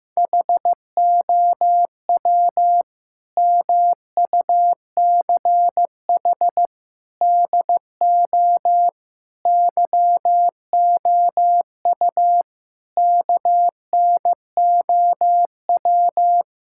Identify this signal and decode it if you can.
Morse code, heard as a keyed tone, about 15 wpm, 700 Hz. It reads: HOW MUCH DO YOU KNOW
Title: Sentences from Top 1000 Words - Review Part 6 - 15wpm